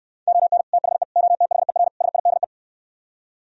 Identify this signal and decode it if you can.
Morse code, heard as a keyed tone, about 68 wpm, 700 Hz. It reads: ON THE OTHER SIDE